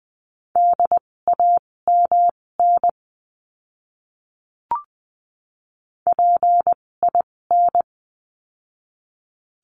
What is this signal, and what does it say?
Morse code, heard as a keyed tone, about 20 wpm, spaced as for 16 wpm, 700 Hz. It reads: DAMN PIN
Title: 256 Lesson 42a - Words 12-20wpm